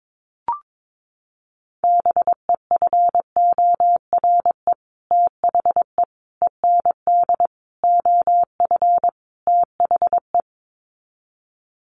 Morse code, heard as a keyed tone, about 22 wpm, 700 Hz. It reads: BEFORE THE END OF THE
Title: Sets of 5 Words Encore 22wpm